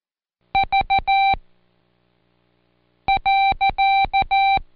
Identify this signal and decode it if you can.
Morse code, heard as a keyed tone, about 14 wpm, spaced as for 4 wpm, 780 Hz. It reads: V.
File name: Code Quick / AudioDownld1 / Seg63-Day7-Pract PeriodMV